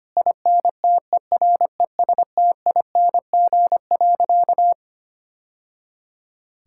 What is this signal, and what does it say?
Morse code, heard as a keyed tone, about 25 wpm, 700 Hz. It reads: INTERESTING.
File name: practice / 25wpm / Sentences with 2-Word Phrases - 25wpm